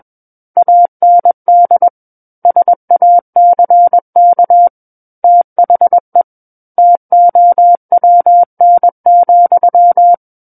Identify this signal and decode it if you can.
Morse code, heard as a keyed tone, about 21 wpm, 700 Hz. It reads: AND SACK THE TOWN,
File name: Book One / 12-Chapter 12 - 21 WPM